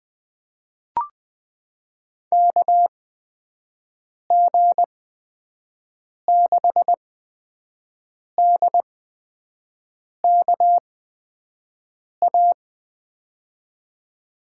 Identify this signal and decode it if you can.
Morse code, heard as a keyed tone, about 21 wpm, spaced as for 5 wpm, 700 Hz. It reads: KG6DKA
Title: USA Call Signs - 6 Characters - 8x Character Spacing - 20wpm